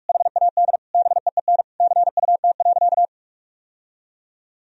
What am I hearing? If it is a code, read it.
Morse code, heard as a keyed tone, about 46 wpm, 700 Hz. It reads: HAD BEEN CUT.
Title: Sentences from Top 1000 Words - Review Part 1 - 45wpm